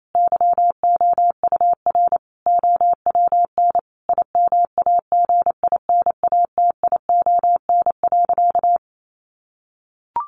Morse code, heard as a keyed tone, about 28 wpm, 700 Hz. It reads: YOUR OWN IMAGINATION.
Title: Sentences from Top 1300 Words 28wpm